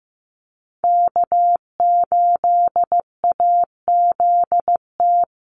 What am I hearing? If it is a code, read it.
Morse code, heard as a keyed tone, about 15 wpm, 700 Hz. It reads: K8AZT
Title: Call Signs Repeated 2x - 15wpm